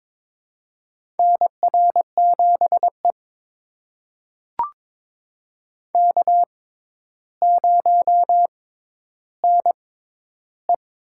Morse code, heard as a keyed tone, about 22 wpm, spaced as for 11 wpm, 700 Hz. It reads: NR7E K0NE
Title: USA Call Signs - 4 Characters - 6x Character Spacing - 22wpm